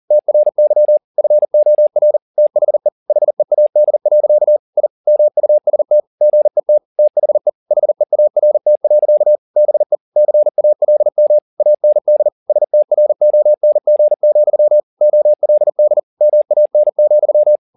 Morse code, heard as a keyed tone, about 40 wpm, 600 Hz. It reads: TRY FOR THE HEAD. I MUST GET THE HEART. BE CALM AND STRONG, OLD MAN,
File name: the old man and the sea 2 40wpm